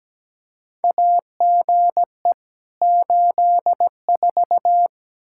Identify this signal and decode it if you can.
Morse code, heard as a keyed tone, about 17 wpm, 700 Hz. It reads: AGE 84
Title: K1USN SST QSO Practice 17wpm